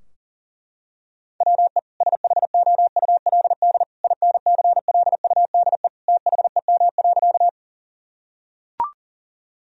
Morse code, heard as a keyed tone, about 40 wpm, 700 Hz. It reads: WE SHOULD INCLUDE THEM.
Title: Sentences from Top 1000 Words - Review Part 3 - 40wpm